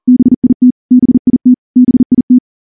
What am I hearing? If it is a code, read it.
Morse code, heard as a keyed tone, about 38 wpm, 250 Hz. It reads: BIT BIT BIT
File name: morse / morse-training-40WPM